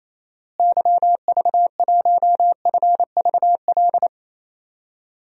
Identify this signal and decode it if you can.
Morse code, heard as a keyed tone, about 28 wpm, 700 Hz. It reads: YV1FVL
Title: DXCC-100 Call Signs - 6 Characters - Repeated 2x - 28wpm